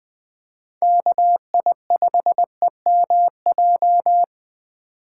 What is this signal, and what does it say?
Morse code, heard as a keyed tone, about 20 wpm, 700 Hz. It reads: KI5EMJ
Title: USA Call Signs - 6 Characters - 2x Character Spacing - 20wpm